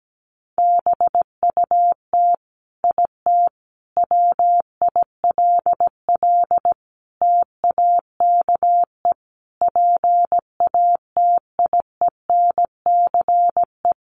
Morse code, heard as a keyed tone, about 17 wpm, 700 Hz. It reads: BUT IT WILL TAKE PATIENCE
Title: Sentences from Top 1300 Words 17wpm